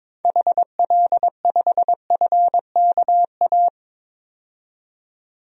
Morse code, heard as a keyed tone, about 22 wpm, 700 Hz. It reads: HL5FKA
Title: DXCC-100 Call Signs - 6 Characters - 22wpm